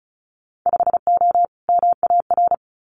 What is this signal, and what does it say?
Morse code, heard as a keyed tone, about 36 wpm, 700 Hz. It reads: 5O MAR
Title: Winter Field Day Exchanges Encore 35wpm